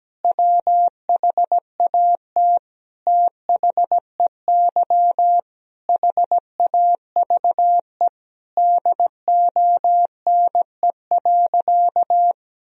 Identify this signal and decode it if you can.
Morse code, heard as a keyed tone, about 17 wpm, 700 Hz. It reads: WHAT THEY HAVE DONE.
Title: Sentences from Top 1000 Words - Review Part 3 - 17wpm